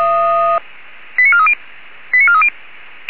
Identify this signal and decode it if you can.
Five-tone selective call (ZVEI), heard as quick runs of five tones, tones of about 80 ms. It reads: 89430 89430